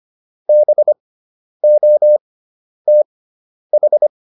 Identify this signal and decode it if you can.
Morse code, heard as a keyed tone, about 25 wpm, spaced as for 10 wpm, 600 Hz. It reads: BOTH